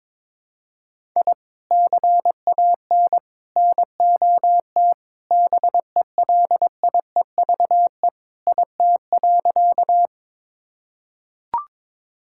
Morse code, heard as a keyed tone, about 21 wpm, 700 Hz. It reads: I CAN NOT BELIEVE IT.
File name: practice / 22wpm / Sentences with 4-Word Phrases - 22wpm